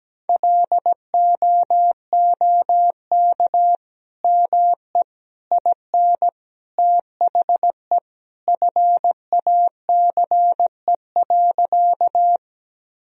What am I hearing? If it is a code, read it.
Morse code, heard as a keyed tone, about 17 wpm, 700 Hz. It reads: LOOK ME IN THE FACE.
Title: Sentences from Top 1000 Words - Review Part 2 - 17wpm